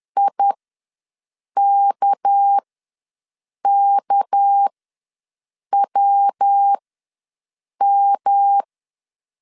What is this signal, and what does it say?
Morse code, heard as a keyed tone, about 10 wpm, spaced as for 6 wpm, 800 Hz. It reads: IKKWM
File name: LEZIONI / 6 PKX0 / 6 PK ws 10wpm 3745